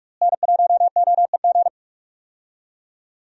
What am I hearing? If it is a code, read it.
Morse code, heard as a keyed tone, about 46 wpm, 700 Hz. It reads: N1OEG
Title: USA Call Signs - 5 Characters - Repeated 2x - 45wpm